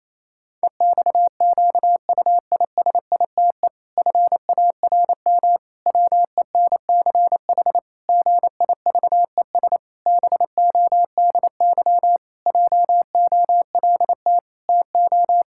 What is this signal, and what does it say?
Morse code, heard as a keyed tone, about 28 wpm, 700 Hz. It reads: EXQUISITE FARM WENCH GIVES BODY JOLT TO
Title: Pangram Challenge v2 - 28wpm